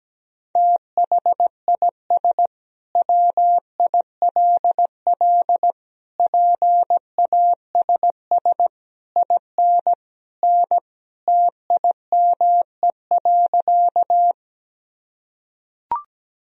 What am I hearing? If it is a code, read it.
Morse code, heard as a keyed tone, about 17 wpm, 700 Hz. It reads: THIS WILL PASS IN N TIME.
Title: Sentences from Top 400 Words and Abbreviations 17wpm